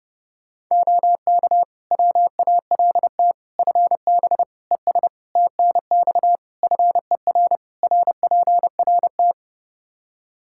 Morse code, heard as a keyed tone, about 32 wpm, 700 Hz. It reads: OK WALT FB ES TNX FER RPRT
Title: Practice QSO - Exchange 2 - Info - 30wpm